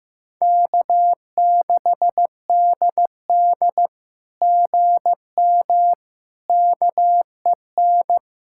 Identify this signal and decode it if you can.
Morse code, heard as a keyed tone, about 16 wpm, 700 Hz. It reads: K6DD GM KEN
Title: SOTA QSO Practice 15wpm